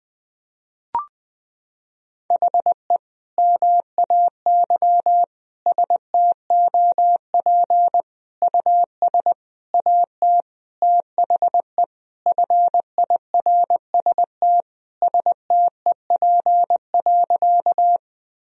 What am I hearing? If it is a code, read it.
Morse code, heard as a keyed tone, about 20 wpm, 700 Hz. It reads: HE MAY STOP US AT THE FIRST STEP.
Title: Sentences from Top 1000 Words - Review Part 2 - 20wpm